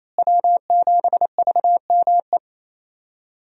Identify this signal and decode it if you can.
Morse code, heard as a keyed tone, about 28 wpm, 700 Hz. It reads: W7VME
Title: USA Call Signs - 5 Characters - Repeated 3x - 28wpm